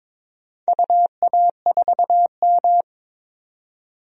Morse code, heard as a keyed tone, about 22 wpm, 700 Hz. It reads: UA4M
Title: DXCC-100 Call Signs - 4 Characters - 22wpm